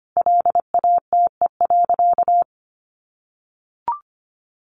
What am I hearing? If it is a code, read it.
Morse code, heard as a keyed tone, about 25 wpm, 700 Hz. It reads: LATE.
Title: Sentences from Top 300 Words 25wpm